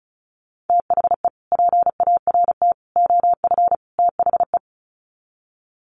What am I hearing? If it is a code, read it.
Morse code, heard as a keyed tone, about 34 wpm, 700 Hz. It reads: THE PART OF THE
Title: Sets of 4 Words 35wpm